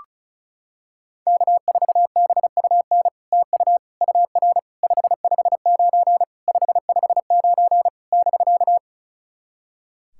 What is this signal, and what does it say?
Morse code, heard as a keyed tone, about 36 wpm, 700 Hz. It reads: K4BUN TU UR 559 559 <BK>
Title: SOTA QSO Practice 35wpm